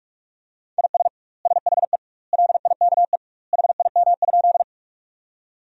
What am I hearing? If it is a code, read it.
Morse code, heard as a keyed tone, about 46 wpm, 700 Hz. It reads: IS SHE LIKE HIM?
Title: 2x Word-Spacing - Sentences from Top 100 Words - 45wpm